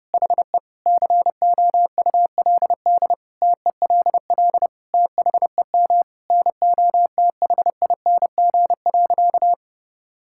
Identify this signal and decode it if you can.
Morse code, heard as a keyed tone, about 32 wpm, 700 Hz. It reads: HE COULD TELL THEM NOTHING.
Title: Sentences from Top 1000 Words - Review Part 6 - 30wpm